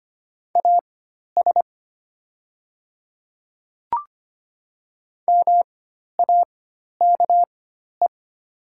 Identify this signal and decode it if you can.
Morse code, heard as a keyed tone, about 25 wpm, spaced as for 12 wpm, 690 Hz. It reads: AS MAKE
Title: 4x Character Spacing - Top 100 Words - 25wpm